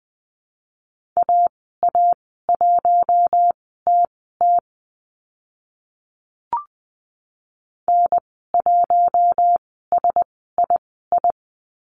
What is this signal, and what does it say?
Morse code, heard as a keyed tone, about 20 wpm, spaced as for 14 wpm, 700 Hz. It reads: AA1TT N1SII
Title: 042 Lesson 8g - Number 1 - Call Signs 20wpm